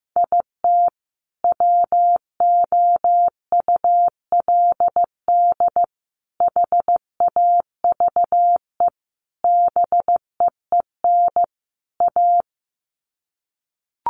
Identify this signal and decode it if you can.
Morse code, heard as a keyed tone, about 15 wpm, 700 Hz. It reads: IT WOULD HAVE BEEN A
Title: Sets of 5 Words Encore 15wpm